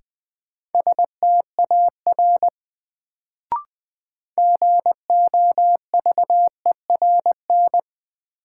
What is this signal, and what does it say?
Morse code, heard as a keyed tone, about 20 wpm, 700 Hz. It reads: STAR GOVERN
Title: Sentences from Top 1000 Words - Review Part 2 - New Word Drill - 20wpm